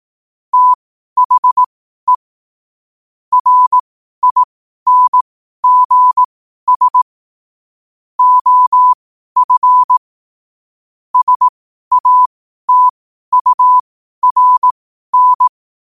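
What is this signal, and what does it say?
Morse code, heard as a keyed tone, about 18 wpm, spaced as for 12 wpm, 1000 Hz. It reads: THE RINGS OF SATURN
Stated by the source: Morse code sent at 12 words per minute.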